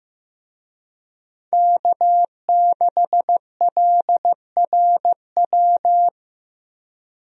Morse code, heard as a keyed tone, about 15 wpm, 700 Hz. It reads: K6LRW
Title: USA Call Signs - 5 Characters - Repeated 2x - 15wpm